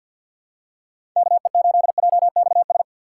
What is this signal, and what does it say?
Morse code, heard as a keyed tone, about 53 wpm, 700 Hz. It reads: KE8JXS
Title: USA Call Signs - 6x Character Spacing - 50wpm